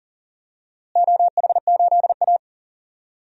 Morse code, heard as a keyed tone, about 38 wpm, 700 Hz. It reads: OH8A
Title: DXCC-100 Call Signs - 4 Characters - Repeated 2x - 40wpm